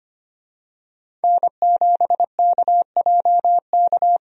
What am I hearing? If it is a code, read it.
Morse code, heard as a keyed tone, about 25 wpm, 690 Hz. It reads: N7KJK
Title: USA Call Signs - 5 Characters - 25wpm